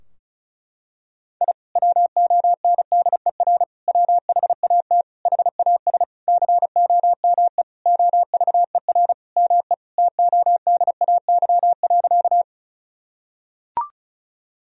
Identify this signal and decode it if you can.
Morse code, heard as a keyed tone, about 36 wpm, 700 Hz. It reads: I WONDER WHAT HAS COME OVER ME TODAY.
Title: Sentences from Top 1000 Words - Review Part 4 - 35wpm